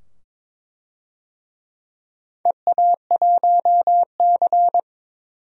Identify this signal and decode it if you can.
Morse code, heard as a keyed tone, about 23 wpm, 700 Hz. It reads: EA1C
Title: DXCC-100 Call Signs - 4 Characters - 22wpm